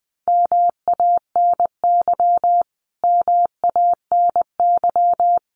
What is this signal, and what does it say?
Morse code, heard as a keyed tone, about 20 wpm, 700 Hz. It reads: MANY MANY
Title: Top 500 Words Repeated 20wpm